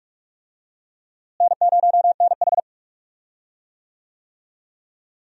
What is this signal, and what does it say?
Morse code, heard as a keyed tone, about 46 wpm, 700 Hz. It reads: N0NH
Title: USA Call Signs - 4 Characters - Repeated 2x - 45wpm